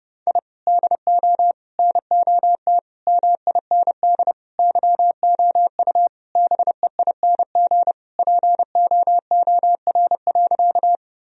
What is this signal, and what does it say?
Morse code, heard as a keyed tone, about 30 wpm, 700 Hz. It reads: I DO NOT MIND YOU BEING POOR.
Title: Sentences from Top 1000 Words - Review Part 3 - 30wpm